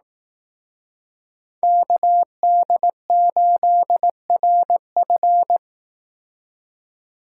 Morse code, heard as a keyed tone, about 18 wpm, 700 Hz. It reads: KD8RF
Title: Speed Racing - USA 2x2 Call Signs 15wpm